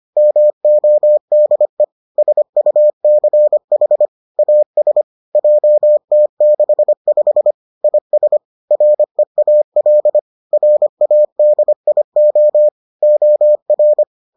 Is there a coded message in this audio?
Yes: Morse code from 0.2 to 14.0 s, heard as a keyed tone, about 25 wpm, 600 Hz. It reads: MODE SUCH AS JT65 IS REAL RADIO OR